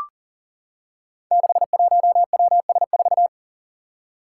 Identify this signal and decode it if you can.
Morse code, heard as a keyed tone, about 43 wpm, 700 Hz. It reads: 61WS4
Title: Five Challenging Letter-Number Combinations 40wpm